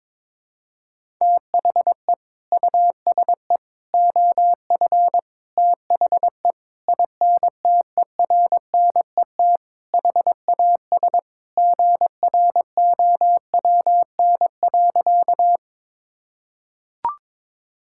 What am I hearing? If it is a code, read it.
Morse code, heard as a keyed tone, about 22 wpm, 700 Hz. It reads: THE USE OF THE INTERNET HAS GROWN.
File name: practice / 22wpm / Sentences with 4-Word Phrases - 22wpm